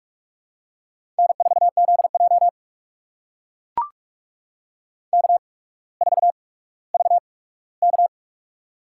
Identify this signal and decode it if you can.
Morse code, heard as a keyed tone, about 44 wpm, spaced as for 19 wpm, 700 Hz. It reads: N4ZJ K4VK
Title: USA Call Signs - 4 Characters - 8x Character Spacing - 45wpm